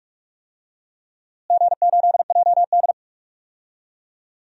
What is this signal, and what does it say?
Morse code, heard as a keyed tone, about 46 wpm, 700 Hz. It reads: G8JD
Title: DXCC-100 Call Signs - 4 Characters - Repeated 2x - 45wpm